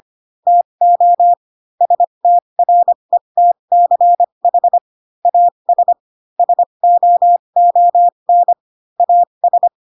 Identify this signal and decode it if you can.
Morse code, heard as a keyed tone, about 25 wpm, spaced as for 22 wpm, 700 Hz. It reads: TO STRETCH AS SOON AS
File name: Pinocchio / Pinocchio ch3 18 25 ik1hge